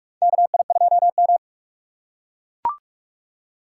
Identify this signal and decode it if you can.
Morse code, heard as a keyed tone, about 43 wpm, 700 Hz. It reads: KI2M
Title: USA Call Signs - 4 Characters - 6x Character Spacing - 45wpm